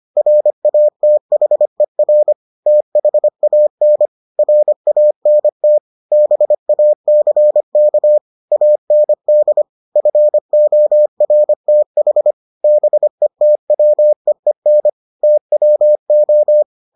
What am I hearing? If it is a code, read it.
Morse code, heard as a keyed tone, about 25 wpm, 600 Hz. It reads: RATHER THAN RANT BACK AND FORTH BETWEEN TWO